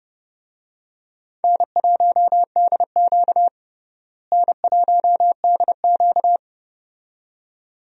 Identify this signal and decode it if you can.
Morse code, heard as a keyed tone, about 30 wpm, 700 Hz. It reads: N1DQ N1DQ
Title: USA Call Signs - 4 Characters - Repeated 2x - 30wpm